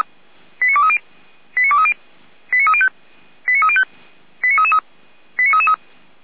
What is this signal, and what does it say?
Five-tone selective call (ZVEI), heard as quick runs of five tones, tones of about 70 ms. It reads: 89230 89230 89305 89305 89333 89333